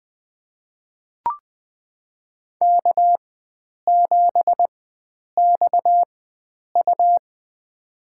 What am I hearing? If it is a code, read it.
Morse code, heard as a keyed tone, about 21 wpm, spaced as for 9 wpm, 700 Hz. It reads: K7XU
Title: USA Call Signs - 4 Characters - 4x Character Spacing - 20wpm